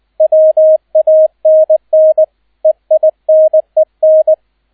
Morse code, heard as a keyed tone, about 20 wpm, 630 Hz, with radio noise. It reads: WANN EINEN